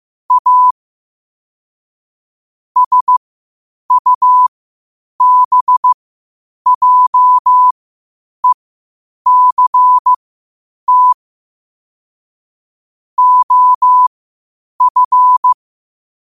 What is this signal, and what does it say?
Morse code, heard as a keyed tone, about 15 wpm, spaced as for 8 wpm, 1000 Hz. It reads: A SUBJECT OF
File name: qotdfiles / 8wpm